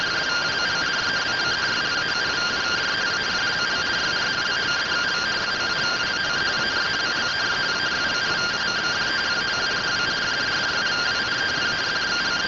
Two-tone teleprinter signal, about 170 Hz shift.